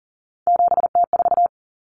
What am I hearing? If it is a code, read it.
Morse code, heard as a keyed tone, about 40 wpm, 700 Hz. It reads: 7T4
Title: Common Cut Numbers - 3-Digit 40wpm